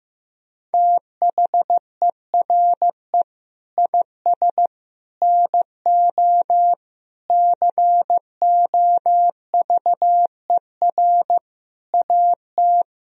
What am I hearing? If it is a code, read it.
Morse code, heard as a keyed tone, about 15 wpm, 700 Hz. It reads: THERE IS NO COVER AT
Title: Sentences from Top 1000 Words - Review Part 1 - 15wpm